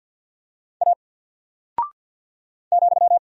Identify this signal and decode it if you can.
Morse code, heard as a keyed tone, about 48 wpm, 700 Hz. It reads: A ,
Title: Single Character (Letter:Number:Punctuation:Prosign) Alt 50wpm